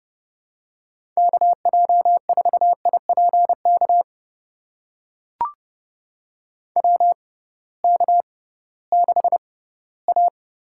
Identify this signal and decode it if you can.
Morse code, heard as a keyed tone, about 32 wpm, spaced as for 18 wpm, 700 Hz. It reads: KJ4IPK WK6A
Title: USA Call Signs - 6x Character Spacing - 30wpm